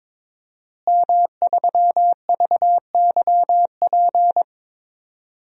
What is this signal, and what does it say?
Morse code, heard as a keyed tone, about 22 wpm, 700 Hz. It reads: M3VYP
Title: DXCC-100 Call Signs - 5 Characters - Repeated 3x - 22wpm